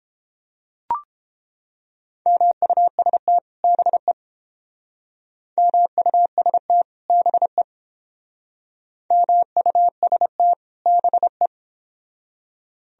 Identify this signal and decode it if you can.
Morse code, heard as a keyed tone, about 29 wpm, 700 Hz. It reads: MUST BE MUST BE MUST BE